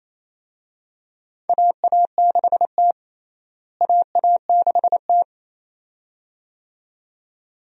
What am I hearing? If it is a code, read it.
Morse code, heard as a keyed tone, about 28 wpm, 700 Hz. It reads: AA6T AA6T
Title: USA Call Signs - 4 Characters - Repeated 2x - 28wpm